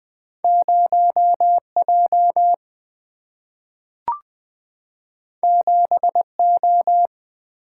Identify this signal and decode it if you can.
Morse code, heard as a keyed tone, about 20 wpm, 700 Hz. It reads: 0J 7O
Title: Single-Letter Single-Number 20wpm